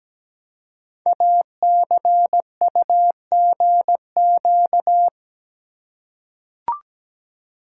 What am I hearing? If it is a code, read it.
Morse code, heard as a keyed tone, about 17 wpm, 700 Hz. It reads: ACUGQ
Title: Code Groups - 5 Letters - 17wpm